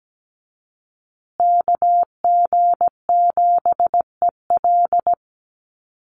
Morse code, heard as a keyed tone, about 17 wpm, 700 Hz. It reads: KG7EL
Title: USA Call Signs Repeated 3x 17wpm